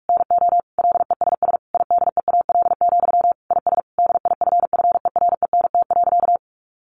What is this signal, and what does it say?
Morse code, heard as a keyed tone, about 44 wpm, 700 Hz. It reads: NO LESS IDEAL, IS DIFFERENT.
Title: Sentences from Top 1400 Words 45wpm